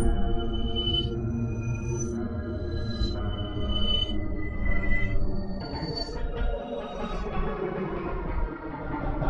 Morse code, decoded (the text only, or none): none